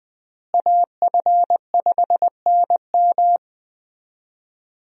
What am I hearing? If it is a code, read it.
Morse code, heard as a keyed tone, about 20 wpm, 700 Hz. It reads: AF5NM
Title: USA Call Signs - 5 Characters - Repeated 2x - 20wpm